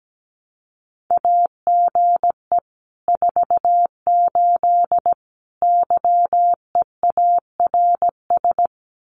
Morse code, heard as a keyed tone, about 17 wpm, 700 Hz. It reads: AGE 48 YEARS
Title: Common QSO Phrases Encore 17wpm